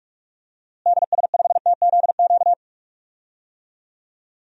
Morse code, heard as a keyed tone, about 41 wpm, 700 Hz. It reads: DS5TZQ
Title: DXCC-100 Call Signs - 6 Characters - Repeated 3x - 45wpm